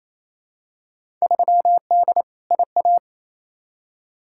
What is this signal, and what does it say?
Morse code, heard as a keyed tone, about 28 wpm, 700 Hz. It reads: 3D IA
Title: Field Day Exchanges Encore - 28wpm